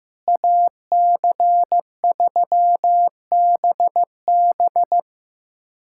Morse code, heard as a keyed tone, about 15 wpm, 700 Hz. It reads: AC3BB
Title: USA Callsign DE Callsign - 5 Characters - 15wpm